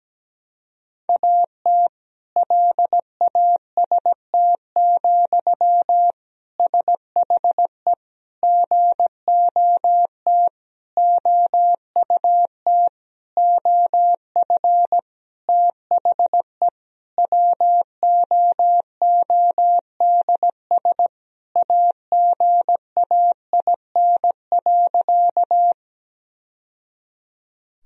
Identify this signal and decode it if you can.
Morse code, heard as a keyed tone, about 17 wpm, 700 Hz. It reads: AT LAST, SHE GOT OUT OF THE WOODS AGAIN.
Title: Sentences from Top 1000 Words - Review Part 2 - 17wpm